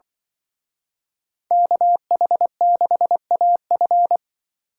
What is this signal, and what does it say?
Morse code, heard as a keyed tone, about 24 wpm, 700 Hz. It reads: KH6AF
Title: Speed Racing - USA 2x2 Call Signs 20wpm